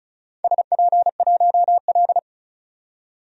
Morse code, heard as a keyed tone, about 34 wpm, 700 Hz. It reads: SP1L
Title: DXCC-100 Call Signs - 4 Characters - 35wpm